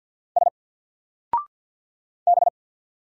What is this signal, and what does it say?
Morse code, heard as a keyed tone, about 50 wpm, 700 Hz. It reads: S B
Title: Single Character (Letter:Number:Punctuation:Prosign) 50wpm